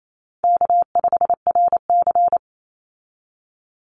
Morse code, heard as a keyed tone, about 28 wpm, 700 Hz. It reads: K5RC
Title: Call Signs - 28wpm